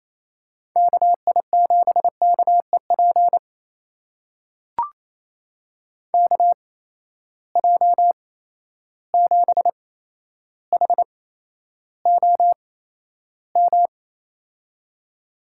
Morse code, heard as a keyed tone, about 28 wpm, spaced as for 12 wpm, 700 Hz. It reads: KI7KEP KJ7HOM
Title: USA Call Signs - 6 Characters - 8x Character Spacing - 28wpm